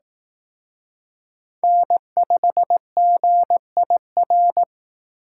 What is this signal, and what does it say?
Morse code, heard as a keyed tone, about 18 wpm, 700 Hz. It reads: N5GIR
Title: Speed Racing - USA 1x3 Call Signs 15wpm